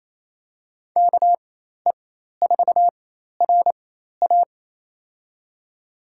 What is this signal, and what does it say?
Morse code, heard as a keyed tone, about 27 wpm, spaced as for 13 wpm, 700 Hz. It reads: KE4RA
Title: USA Call Signs - 4x Character Spacing - 28wpm